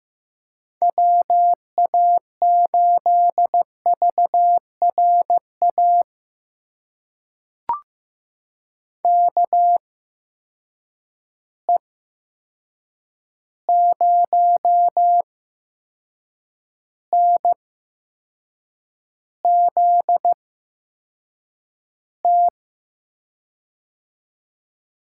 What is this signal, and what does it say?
Morse code, heard as a keyed tone, about 15 wpm, spaced as for 6 wpm, 700 Hz. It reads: WA8VRA KE0NZT
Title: USA Call Signs - 6 Characters - 8x Character Spacing - 15wpm